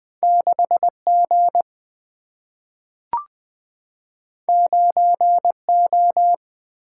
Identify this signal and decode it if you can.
Morse code, heard as a keyed tone, about 20 wpm, 700 Hz. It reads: 6 G 9 O
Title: Single-Letter Single-Number 20wpm